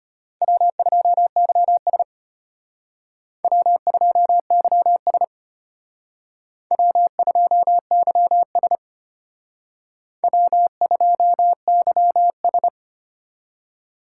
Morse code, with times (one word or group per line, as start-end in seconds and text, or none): 0.4-2.0 W2YS
3.4-5.2 W2YS
6.7-8.8 W2YS
10.2-12.7 W2YS